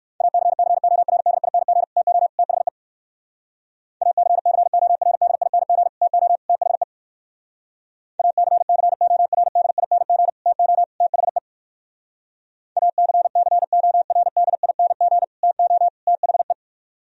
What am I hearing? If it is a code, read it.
Morse code, heard as a keyed tone, about 66 wpm, 700 Hz. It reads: ACCORDING TO THE ACCORDING TO THE ACCORDING TO THE ACCORDING TO THE